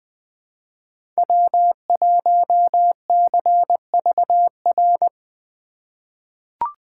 Morse code, heard as a keyed tone, about 20 wpm, 700 Hz. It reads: W1CVR
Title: USA Call Signs - 5 Characters - 3x Character Spacing - 20wpm